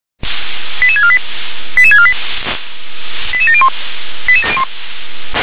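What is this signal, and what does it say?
Five-tone selective call (ZVEI), heard as quick runs of five tones, tones of about 70 ms. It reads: 88648 88648 88701
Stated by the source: ZVEI five-tone selective call received by SDR.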